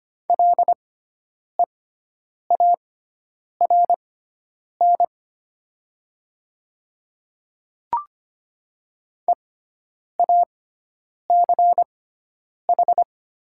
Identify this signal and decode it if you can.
Morse code, heard as a keyed tone, about 25 wpm, spaced as for 9 wpm, 700 Hz. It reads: LEARN EACH
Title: 6x Character Spacing - Top 300 Words - 25wpm